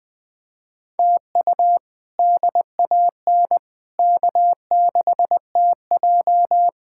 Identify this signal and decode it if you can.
Morse code, heard as a keyed tone, about 20 wpm, 700 Hz. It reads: TU DAN K6TJ
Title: K1USN SST QSO Practice - 20wpm